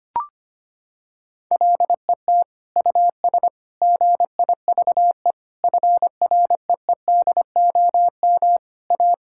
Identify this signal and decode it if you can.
Morse code, heard as a keyed tone, about 25 wpm, 700 Hz. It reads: LET US GIVE FREEDOM A
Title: Sentences from Top 1000 Words - Review Part 5 - 25wpm